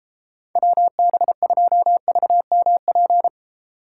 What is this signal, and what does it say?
Morse code, heard as a keyed tone, about 34 wpm, 700 Hz. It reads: WB2VMP